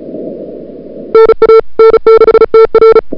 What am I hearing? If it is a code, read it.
Morse code, heard as a keyed tone, about 34 wpm, 440 Hz. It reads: NA N6TR
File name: CWSprint / February 2007 CW Sprint / 01-2007 CW Sprint Hour1